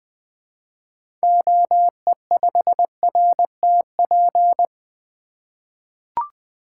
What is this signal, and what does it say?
Morse code, heard as a keyed tone, about 21 wpm, 700 Hz. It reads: OE5RTP
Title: 2023 SOTA Activator Call Signs - 20wpm